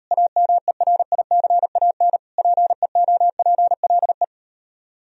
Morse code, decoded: AMERICAN PEOPLE

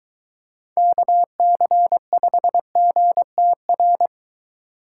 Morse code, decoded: KC5GTR

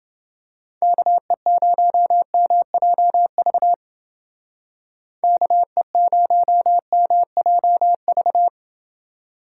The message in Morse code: KE0MJV KE0MJV

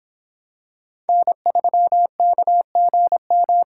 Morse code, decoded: N3KGM

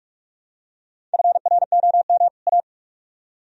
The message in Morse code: FROM A

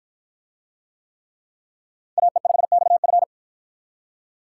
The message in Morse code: AE5XF

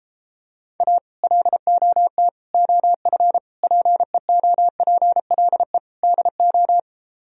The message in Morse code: A LOT OF PEOPLE DO

